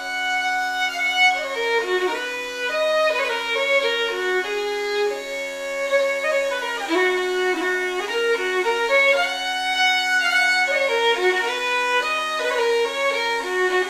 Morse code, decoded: none

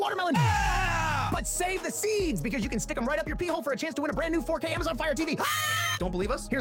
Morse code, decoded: none

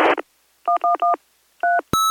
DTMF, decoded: 111 3

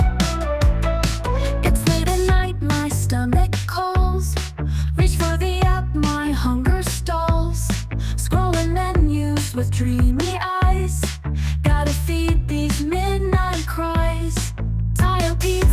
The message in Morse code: none

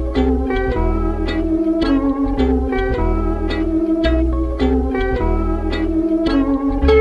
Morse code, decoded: none

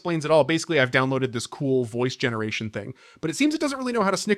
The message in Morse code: none

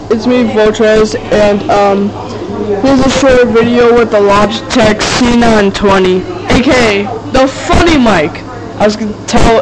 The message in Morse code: none

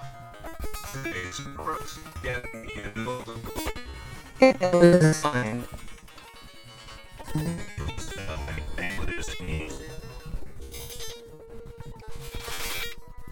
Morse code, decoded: none

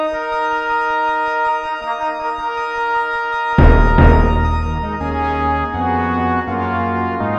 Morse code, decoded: none

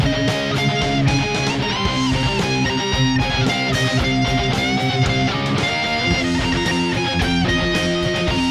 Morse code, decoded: none

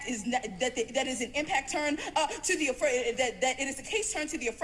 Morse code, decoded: none

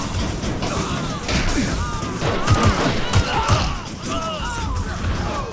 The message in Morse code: none

